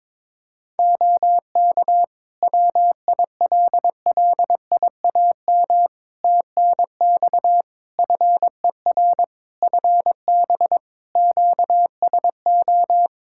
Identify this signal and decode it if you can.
Morse code, heard as a keyed tone, about 22 wpm, 700 Hz. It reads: OK WILLIAM TNX FER FB QSO
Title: Practice QSO - Exchange 4 - Ending - 22wpm